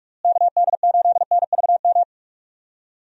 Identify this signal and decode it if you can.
Morse code, heard as a keyed tone, about 46 wpm, 700 Hz. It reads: KD8NVM
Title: USA Call Signs - 6 Characters - Repeated 2x - 45wpm